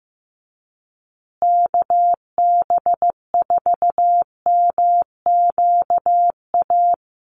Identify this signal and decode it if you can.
Morse code, heard as a keyed tone, about 15 wpm, 700 Hz. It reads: KB4MQA
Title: USA Call Signs Repeated 3x 15wpm